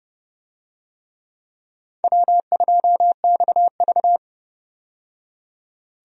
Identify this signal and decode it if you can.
Morse code, heard as a keyed tone, about 32 wpm, 700 Hz. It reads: W2XV
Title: USA Call Signs - 4 Characters - 30wpm